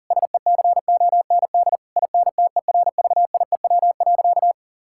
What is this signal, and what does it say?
Morse code, heard as a keyed tone, about 37 wpm, 700 Hz. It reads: SECOND INTERVIEW.
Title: Sentences from Top 1600 Words 40wpm